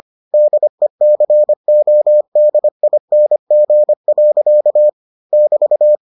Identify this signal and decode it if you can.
Morse code, heard as a keyed tone, about 25 wpm, 600 Hz. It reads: DECODING. <BT>